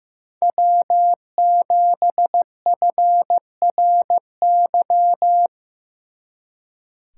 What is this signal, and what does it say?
Morse code, heard as a keyed tone, about 16 wpm, 700 Hz. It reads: W7FRY
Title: Speed Racing - USA 1x3 Call Signs 15wpm